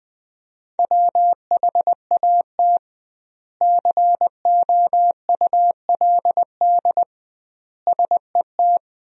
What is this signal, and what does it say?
Morse code, heard as a keyed tone, about 20 wpm, 700 Hz. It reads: WHAT COULD SET
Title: 2x Word-Spacing - Sentences from Top 200 Words - 20wpm